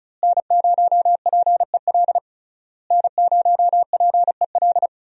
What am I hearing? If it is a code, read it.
Morse code, heard as a keyed tone, about 34 wpm, 700 Hz. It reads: N0PEL N0PEL
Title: USA Call Signs - 5 Characters - Repeated 3x - 35wpm